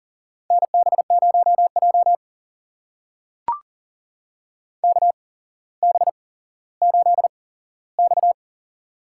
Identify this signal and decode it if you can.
Morse code, heard as a keyed tone, about 40 wpm, spaced as for 17 wpm, 700 Hz. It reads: NB0J KB8X
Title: USA Call Signs - 8x Character Spacing - 40wpm